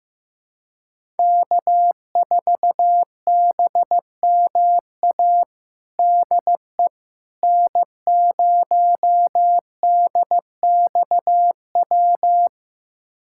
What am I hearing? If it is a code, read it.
Morse code, heard as a keyed tone, about 15 wpm, 700 Hz. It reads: K4BMA DE N0DXW
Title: USA Callsign DE Callsign - 5 Characters - 15wpm